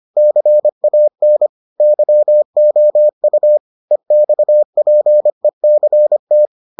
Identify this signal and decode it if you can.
Morse code, heard as a keyed tone, about 25 wpm, 600 Hz. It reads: CAN YOU EXPECT